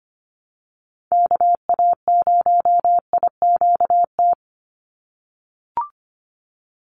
Morse code, heard as a keyed tone, about 25 wpm, 700 Hz. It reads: KA0IQT
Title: Call Signs - 25wpm